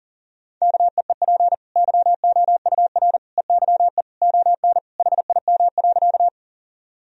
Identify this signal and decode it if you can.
Morse code, heard as a keyed tone, about 40 wpm, 700 Hz. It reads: KEEP YOUR EYE ON HIM.
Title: Sentences from Top 1000 Words - Review Part 1 - 40wpm